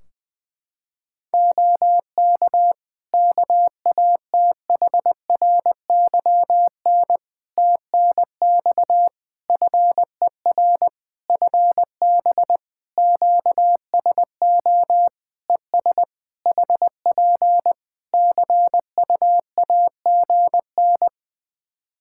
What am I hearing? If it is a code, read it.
Morse code, heard as a keyed tone, about 20 wpm, 700 Hz. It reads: OK KATHRYN TNX FER FB QSO ES HP CUAGN
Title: Practice QSO - Exchange 4 - Ending - 20wpm